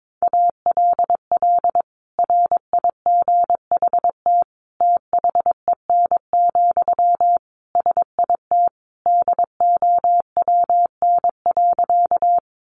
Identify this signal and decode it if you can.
Morse code, heard as a keyed tone, about 22 wpm, 700 Hz. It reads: ALL RIGHT THEN, SIT DOWN.
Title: Sentences from Top 500 Words 22wpm